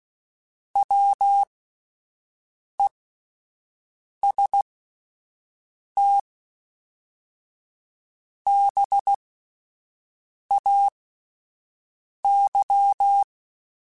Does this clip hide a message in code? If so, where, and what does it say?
Yes: Morse code from 0.8 to 13.2 s, heard as a keyed tone, about 16 wpm, spaced as for 5 wpm, 800 Hz. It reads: WEST BAY